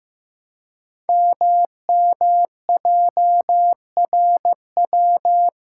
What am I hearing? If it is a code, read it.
Morse code, heard as a keyed tone, about 15 wpm, 700 Hz. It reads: MMJRW
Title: Code Groups - 5 Letters - 15wpm